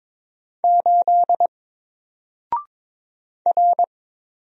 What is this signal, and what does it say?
Morse code, heard as a keyed tone, about 22 wpm, 700 Hz. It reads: 8 R
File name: Single-Letter-Number-22wpm